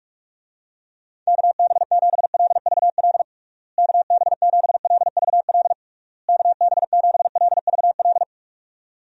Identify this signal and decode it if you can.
Morse code, heard as a keyed tone, about 46 wpm, 700 Hz. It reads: KB7LVL KB7LVL KB7LVL
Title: USA Call Signs - 6 Characters - Repeated 3x - 45wpm